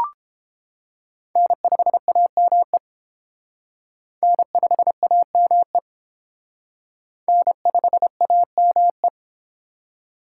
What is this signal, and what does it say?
Morse code, heard as a keyed tone, about 30 wpm, 690 Hz. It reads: N5AME N5AME N5AME